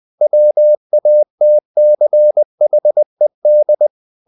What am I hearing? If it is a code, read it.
Morse code, heard as a keyed tone, about 20 wpm, 600 Hz. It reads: WATCHED
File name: the old man and the sea part 2 20wpm